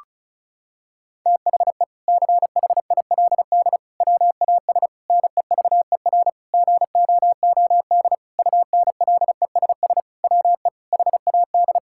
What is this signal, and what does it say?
Morse code, heard as a keyed tone, about 36 wpm, 700 Hz. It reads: THE CHILD WAS NEVER GOOD UNLESS WE HAD
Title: Sentences from Top 1000 Words - Review Part 6 - 35wpm